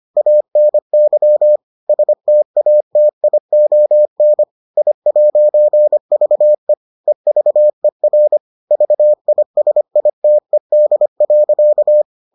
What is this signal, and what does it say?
Morse code, heard as a keyed tone, about 25 wpm, 600 Hz. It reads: ANY STATION I'VE EVER VISITED.